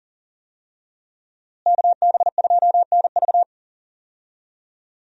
Morse code, heard as a keyed tone, about 40 wpm, 700 Hz. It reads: KB2NV
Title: USA Call Signs - 5 Characters - Repeated 2x - 40wpm